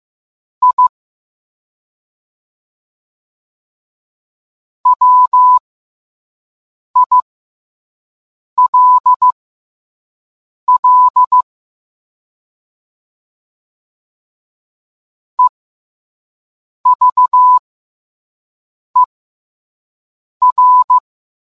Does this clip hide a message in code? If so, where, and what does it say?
Yes: Morse code from 0.6 to 21.0 s, heard as a keyed tone, about 15 wpm, spaced as for 5 wpm, 1000 Hz. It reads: I WILL EVER